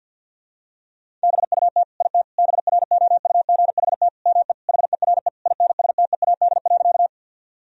Morse code, heard as a keyed tone, about 50 wpm, 700 Hz. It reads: BUT IT BROUGHT ME HERE INSTEAD.
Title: Sentences from Top 1000 Words - Review Part 4 - 50wpm